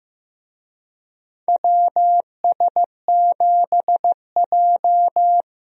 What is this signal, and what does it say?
Morse code, heard as a keyed tone, about 15 wpm, 700 Hz. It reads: WS7J
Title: USA Call Signs - 4 Characters - Repeated 3x - 15wpm